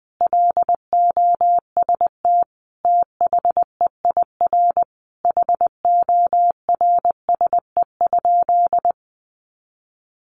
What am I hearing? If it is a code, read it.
Morse code, heard as a keyed tone, about 20 wpm, 700 Hz. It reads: LOST THEIR HORSE?
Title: Sentences from Top 1400 Words 20wpm